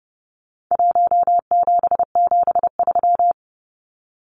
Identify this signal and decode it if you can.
Morse code, heard as a keyed tone, about 30 wpm, 700 Hz. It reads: 1773
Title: Numbers - Contest Style - 4-Digit 30wpm